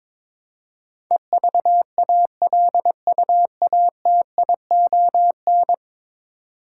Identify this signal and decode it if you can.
Morse code, heard as a keyed tone, about 22 wpm, 700 Hz. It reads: EVALUATION
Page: morse-code-ninja-practice-22wpm